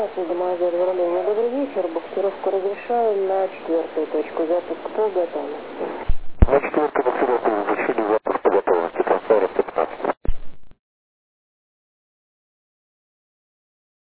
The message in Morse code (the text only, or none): none